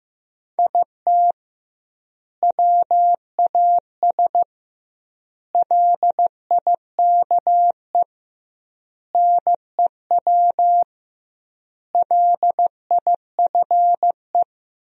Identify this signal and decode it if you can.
Morse code, heard as a keyed tone, about 16 wpm, 700 Hz. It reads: IT WAS LIKE NEW LIFE
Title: 2x Word-Spacing - Sentences from Top 300 Words - 15wpm